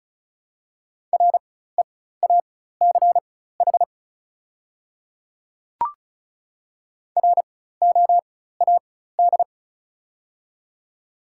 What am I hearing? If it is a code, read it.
Morse code, heard as a keyed tone, about 36 wpm, spaced as for 16 wpm, 700 Hz. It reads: REACH ROAD
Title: 4x Character Spacing - Top 500 Words - 35wpm